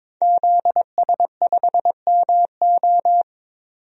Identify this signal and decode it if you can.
Morse code, heard as a keyed tone, about 22 wpm, 700 Hz. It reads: ZS5MO
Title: DXCC-100 Call Signs - 5 Characters - 22wpm